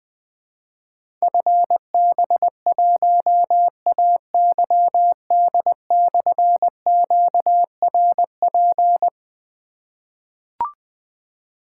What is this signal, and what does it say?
Morse code, heard as a keyed tone, about 20 wpm, 700 Hz. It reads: FB1AYD/QRP
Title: Difficult Call Signs 20wpm